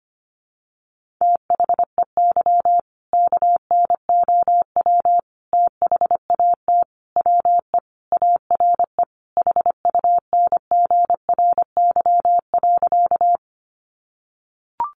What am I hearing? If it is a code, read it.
Morse code, heard as a keyed tone, about 25 wpm, 700 Hz. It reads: THEY KNOW THAT WE ARE HUNGRY.
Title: Sentences from Top 1400 Words 25wpm